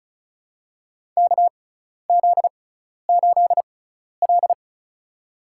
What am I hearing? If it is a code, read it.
Morse code, heard as a keyed tone, about 36 wpm, spaced as for 12 wpm, 690 Hz. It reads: KZ8L
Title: USA Call Signs - 4 Characters - 6x Character Spacing - 35wpm